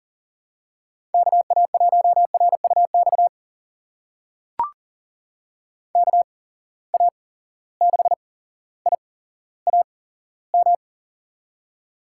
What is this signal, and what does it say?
Morse code, heard as a keyed tone, about 41 wpm, spaced as for 17 wpm, 700 Hz. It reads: KA1RUX KA6IAM
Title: USA Call Signs - 6 Characters - 8x Character Spacing - 40wpm